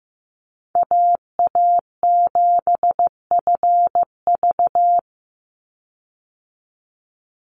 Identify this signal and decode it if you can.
Morse code, heard as a keyed tone, about 15 wpm, 700 Hz. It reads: AA7FV
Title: Call Signs Repeated 3x - 15wpm